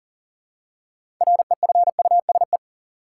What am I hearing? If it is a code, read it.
Morse code, heard as a keyed tone, about 40 wpm, 700 Hz. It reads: REFUSE
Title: Six Letter Words - 40wpm